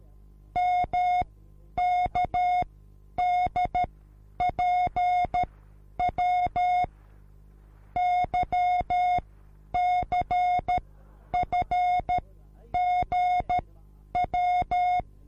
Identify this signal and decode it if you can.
Morse code, heard as a keyed tone, about 13 wpm, spaced as for 9 wpm, 720 Hz. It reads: MKDPW YCFGW